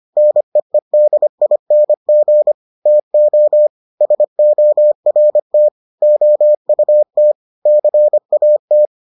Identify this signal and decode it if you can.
Morse code, heard as a keyed tone, about 25 wpm, 600 Hz. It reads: NEEDING TO SORT OUT CAT